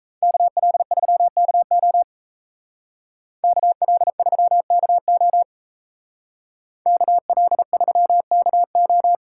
Morse code, heard as a keyed tone, about 33 wpm, 700 Hz. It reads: KL3KO KL3KO KL3KO
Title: Speed Racing - USA 2x2 Call Signs 28wpm